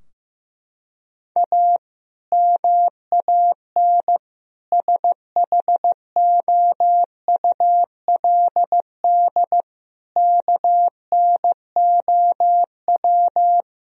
Morse code, heard as a keyed tone, about 15 wpm, 700 Hz. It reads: A MAN SHOULD KNOW
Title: Sentences from Top 1000 Words - Review Part 2 - 15wpm